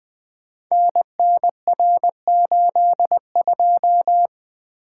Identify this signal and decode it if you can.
Morse code, heard as a keyed tone, about 20 wpm, 700 Hz. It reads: NNR82
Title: Code Groups - 5 Letter:Numbers - 20wpm